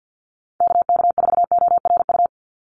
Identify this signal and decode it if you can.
Morse code, heard as a keyed tone, about 48 wpm, 700 Hz. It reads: KK4ORU
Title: USA Call Signs 50wpm